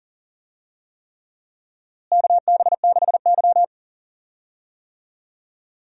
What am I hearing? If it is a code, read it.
Morse code, heard as a keyed tone, about 40 wpm, 700 Hz. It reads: KB6Y
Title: USA Call Signs - 4 Characters - Repeated 3x - 40wpm